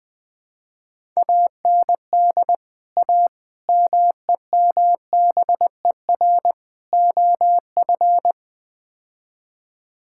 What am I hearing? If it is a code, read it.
Morse code, heard as a keyed tone, about 20 wpm, 700 Hz. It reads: AND A MEMBER OF
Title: Build A Phrase - 5-Word Phrases - 20wpm